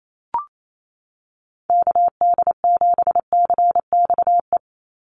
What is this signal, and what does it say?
Morse code, heard as a keyed tone, about 28 wpm, 700 Hz. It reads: KD7CXE
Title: USA Call Signs 28wpm